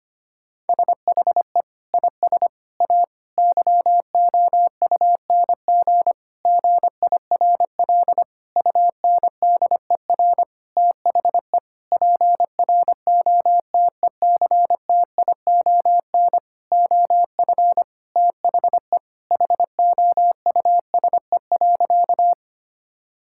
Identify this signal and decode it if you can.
Morse code, heard as a keyed tone, about 25 wpm, 700 Hz. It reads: SHE IS A YOUNG GIRL UNDER THE PROTECTION OF THE HOUSE.
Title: Sentences from Top 1600 Words 25wpm